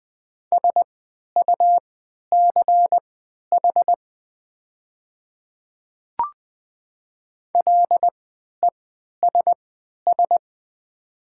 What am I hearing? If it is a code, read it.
Morse code, heard as a keyed tone, about 20 wpm, spaced as for 11 wpm, 700 Hz. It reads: SUCH LESS
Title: 3x Character Spacing - Top 500 Words - 20wpm